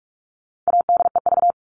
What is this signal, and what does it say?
Morse code, heard as a keyed tone, about 44 wpm, 700 Hz. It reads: ADEV
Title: All Cut Numbers - 4-Digit 45wpm